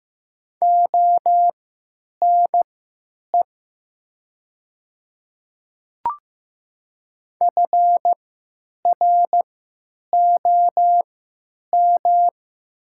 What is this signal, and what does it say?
Morse code, heard as a keyed tone, about 15 wpm, spaced as for 9 wpm, 700 Hz. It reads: ONE FROM
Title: 3x Character Spacing - Top 100 Words - 15wpm